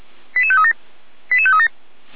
Five-tone selective call (ZVEI), heard as quick runs of five tones, tones of about 70 ms. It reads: 88537 88537
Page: /455/SDR/ZVEI/